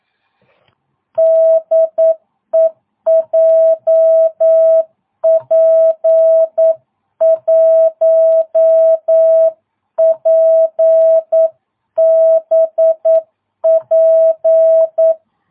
Morse code, heard as a keyed tone, about 9 wpm, 660 Hz. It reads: DE JP1PBP
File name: jh8cbh / 2022cw